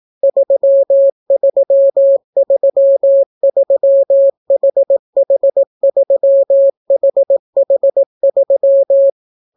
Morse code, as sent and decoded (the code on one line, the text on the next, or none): ...-- ...-- ...-- ...-- .... .... ...-- .... .... ...--
3333HH3HH3